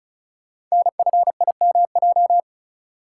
Morse code, heard as a keyed tone, about 36 wpm, 690 Hz. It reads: NFIMJ
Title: Code Groups - 5 Letters - 35wpm